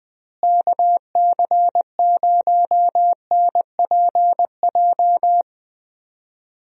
Morse code, heard as a keyed tone, about 21 wpm, 700 Hz. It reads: KC0NPJ
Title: USA Call Signs - 6 Characters - Repeated 3x - 20wpm